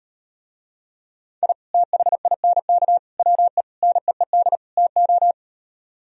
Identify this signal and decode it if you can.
Morse code, heard as a keyed tone, about 39 wpm, 700 Hz. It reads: I THINK WE NEED TO